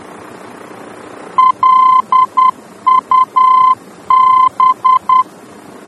Morse code, heard as a keyed tone, about 10 wpm, 1030 Hz. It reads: LUB
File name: germany / mx / LUB-D-VOR